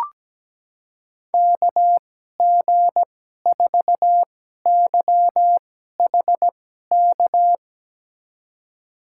Morse code, heard as a keyed tone, about 17 wpm, spaced as for 12 wpm, 700 Hz. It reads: KG4YHK
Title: USA Call Signs - 2x Character Spacing - 17wpm